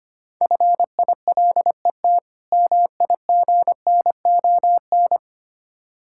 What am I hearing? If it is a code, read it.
Morse code, heard as a keyed tone, about 25 wpm, 700 Hz. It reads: FILET MIGNON
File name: practice / 25wpm / Difficult Words with Hints - 25wpm